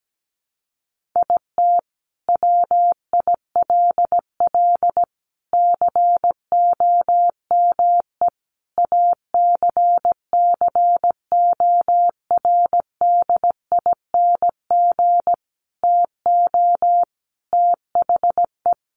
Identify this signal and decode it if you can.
Morse code, heard as a keyed tone, about 17 wpm, 700 Hz. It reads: IT WILL COME ACCORDING TO THE
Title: Sentences from Top 1400 Words 17wpm